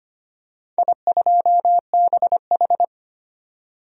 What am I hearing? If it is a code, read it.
Morse code, heard as a keyed tone, about 25 wpm, 700 Hz. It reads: I2BH
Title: DXCC-100 Call Signs - 4 Characters - 25wpm